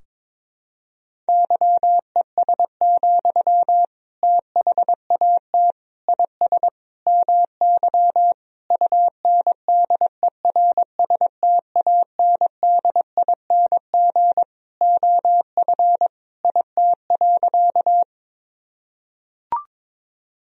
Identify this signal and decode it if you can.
Morse code, heard as a keyed tone, about 22 wpm, 700 Hz. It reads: YES, THAT IS MY UNDERSTANDING OF IT.
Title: Sentences from Top 1000 Words - Review Part 3 - 22wpm